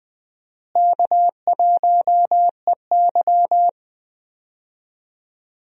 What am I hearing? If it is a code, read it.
Morse code, heard as a keyed tone, about 20 wpm, 700 Hz. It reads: K1EY
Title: USA Call Signs - 4 Characters - Repeated 2x - 20wpm